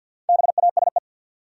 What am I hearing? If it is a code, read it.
Morse code, heard as a keyed tone, about 53 wpm, 700 Hz. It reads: BASE
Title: Four Letter Words - 50wpm